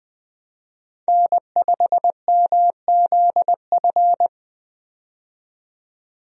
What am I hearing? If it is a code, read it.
Morse code, heard as a keyed tone, about 20 wpm, 700 Hz. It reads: N5MZF
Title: USA Call Signs - 5 Characters - 20wpm